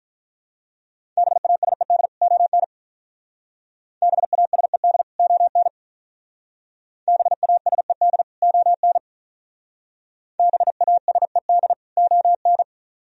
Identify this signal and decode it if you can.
Morse code, heard as a keyed tone, about 43 wpm, 700 Hz. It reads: BASED ON BASED ON BASED ON BASED ON
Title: Speed-Racing - Sets of 2 Words 35wpm